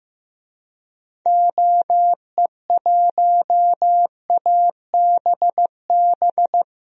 Morse code, heard as a keyed tone, about 15 wpm, 700 Hz. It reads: OE1ABB
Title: DXCC-100 Call Signs - 6 Characters - 15wpm